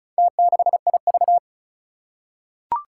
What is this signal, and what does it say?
Morse code, heard as a keyed tone, about 36 wpm, 700 Hz. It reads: T6IV
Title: Four Challenging Letter-Number Combinations 35wpm